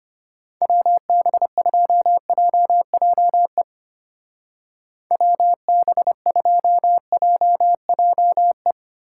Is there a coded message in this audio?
Yes: Morse code from 0.6 to 8.7 s, heard as a keyed tone, about 32 wpm, 700 Hz. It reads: WB2JJE WB2JJE